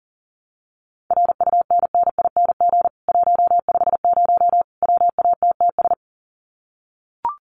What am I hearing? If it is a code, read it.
Morse code, heard as a keyed tone, about 40 wpm, 700 Hz. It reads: RUNNING 150 WATTS
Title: Common QSO Phrases 40wpm